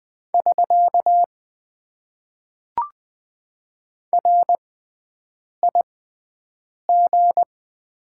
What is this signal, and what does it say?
Morse code, heard as a keyed tone, about 20 wpm, spaced as for 7 wpm, 700 Hz. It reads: <SK> RIG
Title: Top 90 QSO Elements - 6x Character Spacing - 20wpm